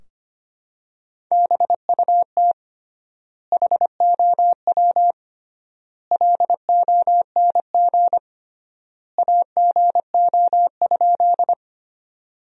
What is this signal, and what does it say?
Morse code, heard as a keyed tone, about 25 wpm, 700 Hz. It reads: BUT HOW LONG AGO?
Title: 3x Word-Spacing - Sentences from Top 500 Words - 25wpm